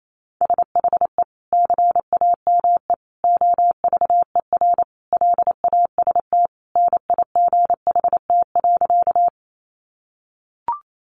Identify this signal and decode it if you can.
Morse code, heard as a keyed tone, about 28 wpm, 700 Hz. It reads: SHE CAME OVER LAST NIGHT.
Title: Sentences from Top 300 Words 28wpm